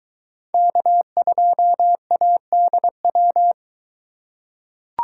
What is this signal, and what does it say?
Morse code, heard as a keyed tone, about 21 wpm, 700 Hz. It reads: K2ADW